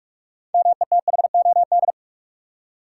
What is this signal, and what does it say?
Morse code, heard as a keyed tone, about 46 wpm, 700 Hz. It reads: METHOD
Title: Six Letter Words - 45wpm